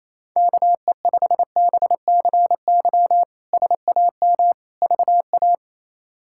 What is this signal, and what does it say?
Morse code, heard as a keyed tone, about 28 wpm, 700 Hz. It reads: KE5BCY SAM VA
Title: K1USN SST QSO Practice - 28wpm